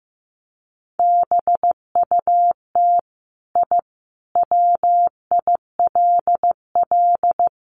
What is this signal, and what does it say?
Morse code, heard as a keyed tone, about 15 wpm, 700 Hz. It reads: BUT I WILL
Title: Sentences from Top 1300 Words 15wpm